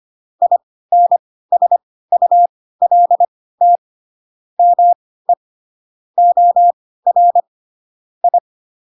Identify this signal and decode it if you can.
Morse code, heard as a keyed tone, about 25 wpm, spaced as for 16 wpm, 700 Hz. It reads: INSULT ME OR I
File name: Pinocchio / Pinocchio ch2 10 25 ik1hge